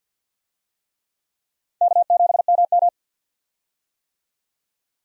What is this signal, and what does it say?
Morse code, heard as a keyed tone, about 49 wpm, 700 Hz. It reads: K7MM
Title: USA Call Signs - 4 Characters - 50wpm